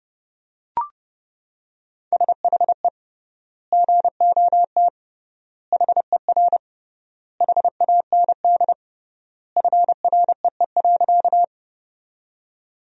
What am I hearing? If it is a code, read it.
Morse code, heard as a keyed tone, about 32 wpm, 700 Hz. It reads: SHE GOT HER HAND FREE.
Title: 3x Word-Spacing - Sentences from Top 500 Words - 30wpm